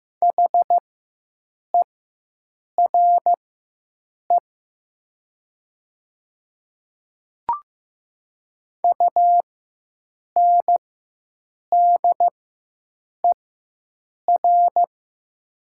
Morse code, heard as a keyed tone, about 15 wpm, spaced as for 7 wpm, 700 Hz. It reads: HERE UNDER
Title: 4x Character Spacing - Top 200 Words - 15wpm